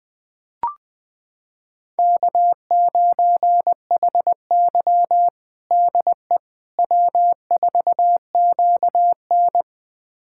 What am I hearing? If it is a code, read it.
Morse code, heard as a keyed tone, about 21 wpm, 700 Hz. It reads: K9HY DE W4QN
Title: USA Callsign DE Callsign - 4 Characters - 20wpm